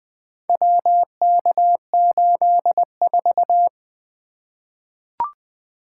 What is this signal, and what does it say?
Morse code, heard as a keyed tone, about 20 wpm, 700 Hz. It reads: WK84
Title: Four Letter-Number Combinations - 20wpm